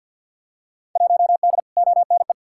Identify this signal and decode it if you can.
Morse code, heard as a keyed tone, about 50 wpm, 700 Hz. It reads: 1D ONE
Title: Field Day Exchanges - 50wpm